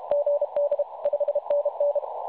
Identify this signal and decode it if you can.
Morse code, heard as a keyed tone, about 31 wpm, 560 Hz. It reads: GD 5NN
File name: s500e 3523.9kHz